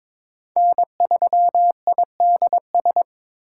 Morse code, heard as a keyed tone, about 22 wpm, 700 Hz. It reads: N3IDS
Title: USA Call Signs - 5 Characters - 4x Character Spacing - 22wpm